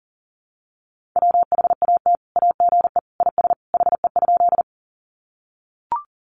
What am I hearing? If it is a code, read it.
Morse code, heard as a keyed tone, about 40 wpm, 700 Hz. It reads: WHAT AGE IS HE?
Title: Sentences from Top 500 Words 40wpm